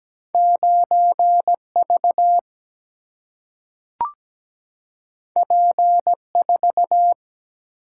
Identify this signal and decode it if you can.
Morse code, heard as a keyed tone, about 17 wpm, 700 Hz. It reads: 9V P4
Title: Single-Letter Single-Number 17wpm